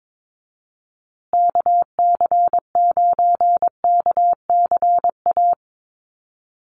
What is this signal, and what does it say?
Morse code, heard as a keyed tone, about 22 wpm, 700 Hz. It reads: KC9KCA
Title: USA Call Signs Repeated 3x 22wpm